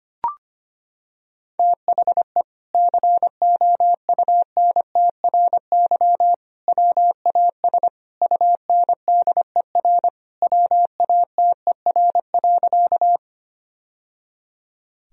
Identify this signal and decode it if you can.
Morse code, heard as a keyed tone, about 25 wpm, 700 Hz. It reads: THE COUNTRY WAS UNDER WATER.
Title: Sentences from Top 1000 Words - Review Part 1 - 25wpm